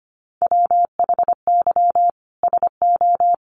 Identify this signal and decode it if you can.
Morse code, heard as a keyed tone, about 25 wpm, 700 Hz. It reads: WHY SO
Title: Sentences from Top 1200 Words 25wpm